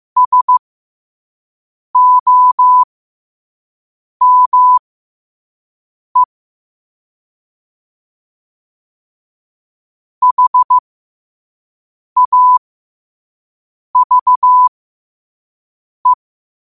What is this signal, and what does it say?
Morse code, heard as a keyed tone, about 15 wpm, spaced as for 5 wpm, 1000 Hz. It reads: SOME HAVE
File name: qotdfiles / 5wpm